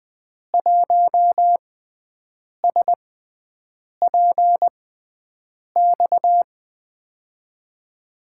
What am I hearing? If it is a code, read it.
Morse code, heard as a keyed tone, about 20 wpm, spaced as for 7 wpm, 700 Hz. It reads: 1SPX
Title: USA Call Signs - 6x Character Spacing - 20wpm